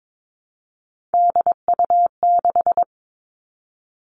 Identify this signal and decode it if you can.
Morse code, heard as a keyed tone, about 22 wpm, 700 Hz. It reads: DU6
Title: Three Challenging Letter-Number Combinations 22wpm